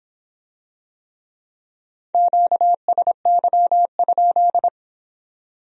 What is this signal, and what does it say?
Morse code, heard as a keyed tone, about 25 wpm, 700 Hz. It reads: QSY?